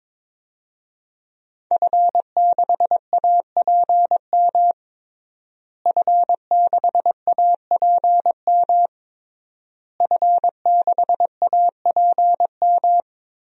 Morse code, heard as a keyed tone, about 21 wpm, 700 Hz. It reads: F6APM F6APM F6APM
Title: DXCC-100 Call Signs - 5 Characters - Repeated 3x - 22wpm